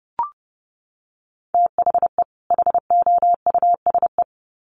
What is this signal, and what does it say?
Morse code, heard as a keyed tone, about 28 wpm, 700 Hz. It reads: THE HOUSE
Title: Sets of 2 Words 30wpm